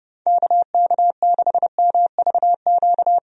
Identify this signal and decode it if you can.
Morse code, heard as a keyed tone, about 32 wpm, 700 Hz. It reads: KK6MVQ
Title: USA Call Signs - 6 Characters - Repeated 3x - 30wpm